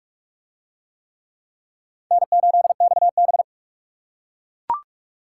Morse code, heard as a keyed tone, about 44 wpm, 700 Hz. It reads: N8XB